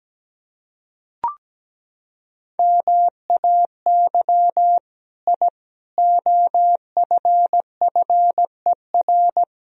Morse code, heard as a keyed tone, about 17 wpm, 700 Hz. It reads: MAY I OFFER
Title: Sentences from Top 1000 Words - Review Part 4 - 17wpm